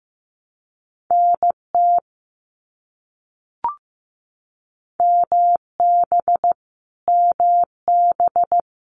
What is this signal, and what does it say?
Morse code, heard as a keyed tone, about 15 wpm, 700 Hz. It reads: NT MB MB
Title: Canadian Provinces and Territories - Repeated 3x - 15wpm